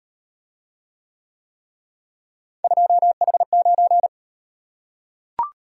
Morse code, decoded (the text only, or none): 2H9